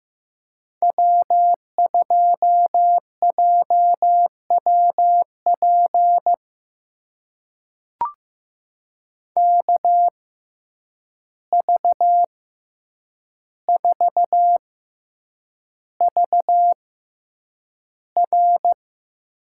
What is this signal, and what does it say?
Morse code, heard as a keyed tone, about 15 wpm, spaced as for 8 wpm, 700 Hz. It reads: W2JWP KV4VR
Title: USA Call Signs - 5 Characters - 6x Character Spacing - 15wpm